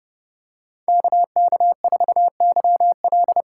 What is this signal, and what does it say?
Morse code, heard as a keyed tone, about 30 wpm, 700 Hz. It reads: KK4YL
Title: USA Call Signs - 6 Characters - 6x Character Spacing - 30wpm